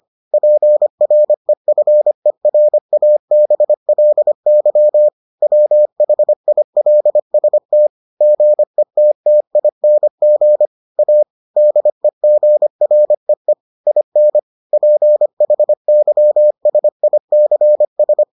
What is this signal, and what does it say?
Morse code, heard as a keyed tone, about 25 wpm, 600 Hz. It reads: PREFERABLY WHILST GETTING A DEGREE IN PHYSICS